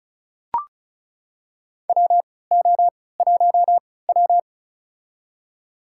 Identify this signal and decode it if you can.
Morse code, heard as a keyed tone, about 36 wpm, spaced as for 20 wpm, 700 Hz. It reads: WO1W
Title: USA Call Signs - 3x Character Spacing - 35wpm